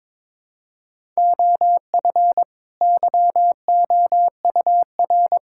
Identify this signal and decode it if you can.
Morse code, heard as a keyed tone, about 22 wpm, 700 Hz. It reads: OF YOUR
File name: Build A Phrase - 2-Word Phrases - 22wpm